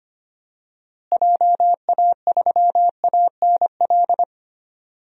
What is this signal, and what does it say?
Morse code, heard as a keyed tone, about 25 wpm, 700 Hz. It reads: JA3ANL
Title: DXCC-100 Call Signs - 6 Characters - Repeated 2x - 25wpm